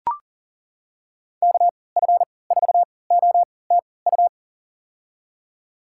Morse code, heard as a keyed tone, about 40 wpm, spaced as for 23 wpm, 700 Hz. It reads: KF4OTU
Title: USA Call Signs - 6 Characters - 3x Character Spacing - 40wpm